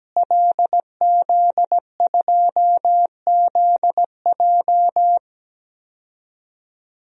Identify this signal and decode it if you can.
Morse code, heard as a keyed tone, about 17 wpm, 700 Hz. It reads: LZ2ZJ
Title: DXCC-100 Call Signs - 5 Characters - Repeated 2x - 17wpm